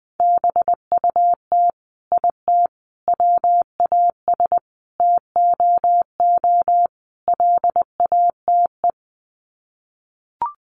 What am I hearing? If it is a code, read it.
Morse code, heard as a keyed tone, about 20 wpm, 700 Hz. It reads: BUT IT WAS TOO LATE
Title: Sets of 5 Words Encore2 20wpm